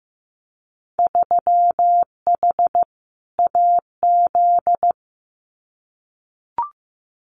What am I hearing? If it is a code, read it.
Morse code, heard as a keyed tone, about 15 wpm, 700 Hz. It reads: 3H AZ
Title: Winter Field Day Exchanges Encore 15wpm